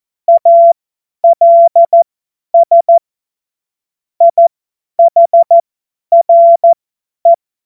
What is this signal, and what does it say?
Morse code, heard as a keyed tone, about 14 wpm, spaced as for 10 wpm, 680 Hz. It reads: ALS IHRE